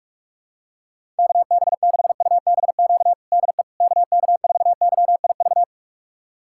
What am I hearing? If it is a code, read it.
Morse code, heard as a keyed tone, about 46 wpm, 700 Hz. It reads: KB6UBQ DE KK4YIV
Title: USA Callsign DE Callsign - 6 Characters - 45wpm